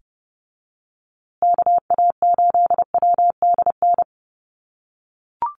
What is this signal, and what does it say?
Morse code, heard as a keyed tone, about 30 wpm, 700 Hz. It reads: KA8WDN
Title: USA Call Signs Repeated 3x 30wpm